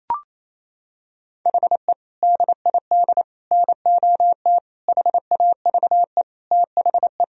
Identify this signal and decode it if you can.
Morse code, heard as a keyed tone, about 28 wpm, 700 Hz. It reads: HE DID NOT HAVE THE
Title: Sentences from Top 1000 Words - Review Part 1 - 28wpm